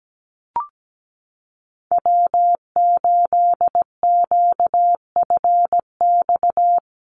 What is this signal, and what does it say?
Morse code, heard as a keyed tone, about 17 wpm, 700 Hz. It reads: W8QFX
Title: USA Call Signs Repeated 2x 17wpm